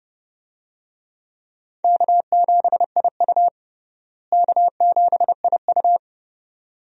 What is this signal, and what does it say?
Morse code, heard as a keyed tone, about 30 wpm, 700 Hz. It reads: K7IU K7IU
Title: USA Call Signs - 4 Characters - Repeated 2x - 30wpm